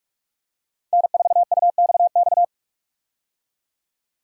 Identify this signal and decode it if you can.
Morse code, heard as a keyed tone, about 47 wpm, 700 Hz. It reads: N4UXX
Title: USA Call Signs - 5 Characters - 45wpm